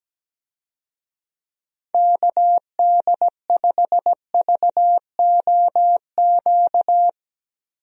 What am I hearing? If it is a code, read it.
Morse code, heard as a keyed tone, about 17 wpm, 700 Hz. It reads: KD5VOQ
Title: USA Call Signs - 6 Characters - Repeated 2x - 17wpm